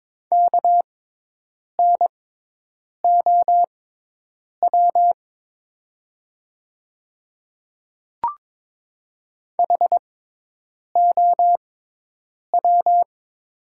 Morse code, heard as a keyed tone, about 23 wpm, spaced as for 8 wpm, 700 Hz. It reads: KNOW HOW
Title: 6x Character Spacing - Top 100 Words - 22wpm